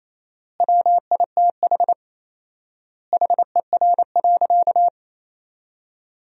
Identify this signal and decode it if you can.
Morse code, heard as a keyed tone, about 27 wpm, 700 Hz. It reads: WITH HER.
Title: 4x Word-Spacing - Sentences from Top 100 Words - 28wpm